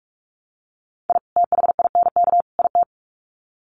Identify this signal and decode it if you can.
Morse code, heard as a keyed tone, about 44 wpm, 700 Hz. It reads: I THINK IT
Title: Sets of 3 Words 45wpm